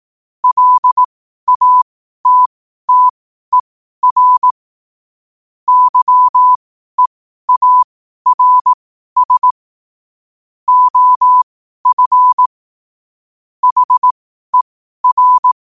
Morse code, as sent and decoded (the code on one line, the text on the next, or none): .-.. .- - - . .-. / -.-- . .- .-. ... / --- ..-. / .... .-.
LATTER YEARS OF HER